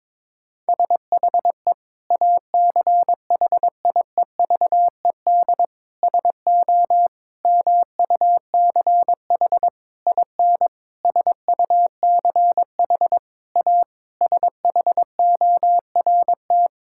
Morse code, heard as a keyed tone, about 22 wpm, 700 Hz. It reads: SHE ACHIEVED SO MUCH IN SUCH A SHORT
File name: Sentences with 7-Word Phrases - 22wpm